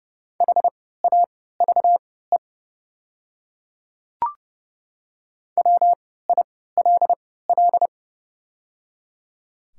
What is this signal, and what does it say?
Morse code, heard as a keyed tone, about 30 wpm, spaced as for 17 wpm, 700 Hz. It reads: HAVE WILL
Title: 3x Character Spacing - Top 100 Words - 30wpm